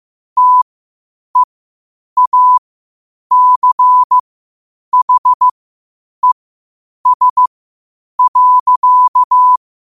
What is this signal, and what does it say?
Morse code, heard as a keyed tone, about 15 wpm, spaced as for 8 wpm, 1000 Hz. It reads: TEACHES.
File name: qotdfiles / 8wpm